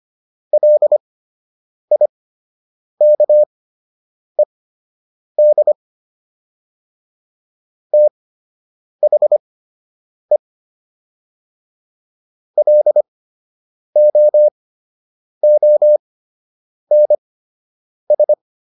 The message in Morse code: LIKED THE LOONS

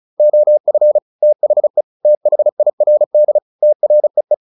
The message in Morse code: OF THE THIRD TREE

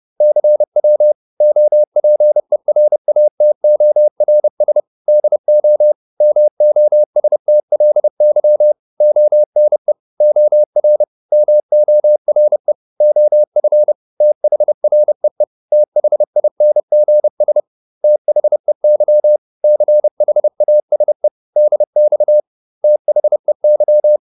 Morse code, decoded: CW OPERATORS DO MOSTLY ONE OR MORE OF THREE THINGS THEY CHASE DX THEY

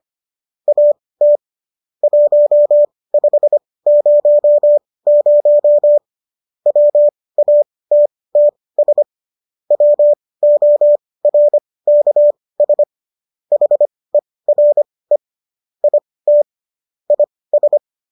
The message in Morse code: AT 1500 WATTS WORKS HERE IT IS

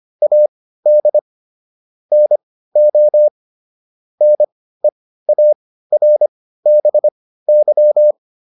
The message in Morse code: AD NO NEARBY

